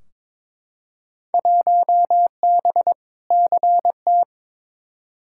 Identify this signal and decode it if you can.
Morse code, heard as a keyed tone, about 22 wpm, 700 Hz. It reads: 1B CT
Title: Field Day Exchanges Encore - 22wpm